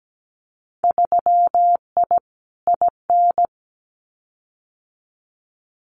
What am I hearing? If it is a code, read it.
Morse code, heard as a keyed tone, about 17 wpm, 700 Hz. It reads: 3I IN
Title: Winter Field Day Exchanges 17wpm